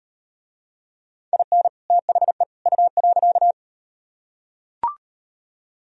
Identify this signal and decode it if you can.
Morse code, heard as a keyed tone, about 34 wpm, 700 Hz. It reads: IN THE U.